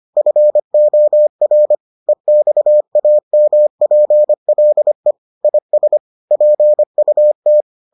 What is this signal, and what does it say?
Morse code, heard as a keyed tone, about 25 wpm, 600 Hz. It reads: FOR EXAMPLE IS PUT